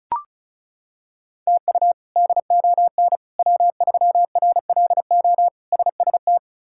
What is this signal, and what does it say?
Morse code, heard as a keyed tone, about 36 wpm, 700 Hz. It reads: TU DON W3RLO SST
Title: K1USN SST QSO Practice - 35wpm